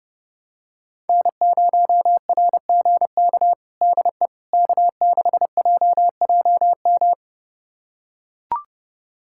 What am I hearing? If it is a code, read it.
Morse code, heard as a keyed tone, about 30 wpm, 700 Hz. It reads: N0RGK DE K6JJM
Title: USA Callsign DE Callsign - 5 Characters - 30wpm